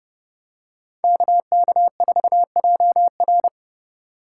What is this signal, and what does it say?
Morse code, heard as a keyed tone, about 29 wpm, 700 Hz. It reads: KK4JR